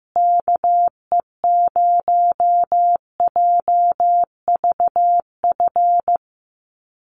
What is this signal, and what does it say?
Morse code, heard as a keyed tone, about 15 wpm, 700 Hz. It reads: KE0JVF
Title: USA Call Signs Repeated 2x 15wpm